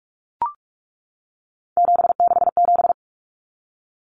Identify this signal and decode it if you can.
Morse code, heard as a keyed tone, about 44 wpm, 700 Hz. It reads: 767
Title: Numbers - 3-Digit 45wpm